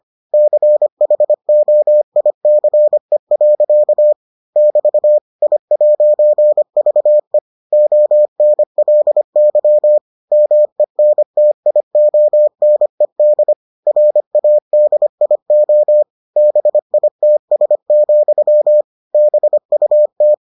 Morse code, heard as a keyed tone, about 25 wpm, 600 Hz. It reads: CHOICE. <BT> I'VE ONLY MENTIONED RADIO BITS, BUT